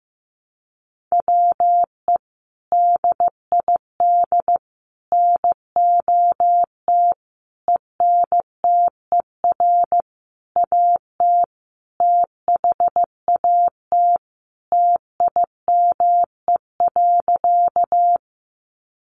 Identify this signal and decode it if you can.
Morse code, heard as a keyed tone, about 15 wpm, 700 Hz. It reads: WE DID NOT ENTER AT THAT TIME.
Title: Sentences from Top 1300 Words 15wpm